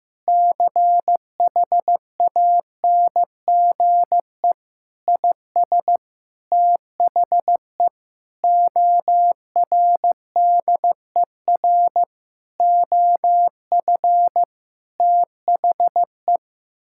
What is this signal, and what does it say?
Morse code, heard as a keyed tone, about 15 wpm, 700 Hz. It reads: CHANGE IS THE ORDER OF THE
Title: Sentences from Top 1000 Words - Review Part 5 - 15wpm